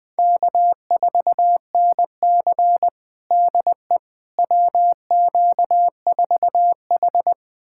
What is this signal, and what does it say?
Morse code, heard as a keyed tone, about 20 wpm, 700 Hz. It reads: K4NC DE WQ4H
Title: USA Callsign DE Callsign - 4 Characters - 20wpm